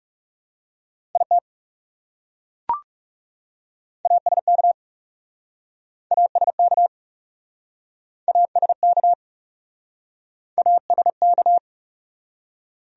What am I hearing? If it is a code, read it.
Morse code, heard as a keyed tone, about 40 wpm, 700 Hz. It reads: IT ASK ASK ASK ASK